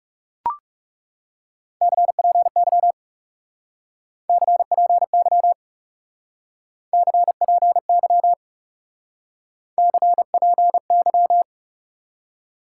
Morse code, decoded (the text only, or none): CPY CPY CPY CPY